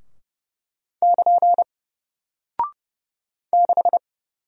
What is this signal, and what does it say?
Morse code, heard as a keyed tone, about 30 wpm, 700 Hz. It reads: <KN> 6
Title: Single Character (Letter:Number:Punctuation:Prosign) Alt 30wpm